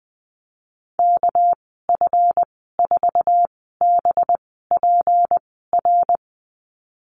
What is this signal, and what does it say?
Morse code, heard as a keyed tone, about 20 wpm, spaced as for 14 wpm, 700 Hz. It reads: KF4BPR
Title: 245 Lesson 39g - Letter z - Call Signs 20wpm